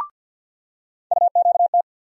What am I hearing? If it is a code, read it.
Morse code, heard as a keyed tone, about 47 wpm, 690 Hz. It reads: UQT
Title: USA Call Sign Postfixes - 3 Letters 50wpm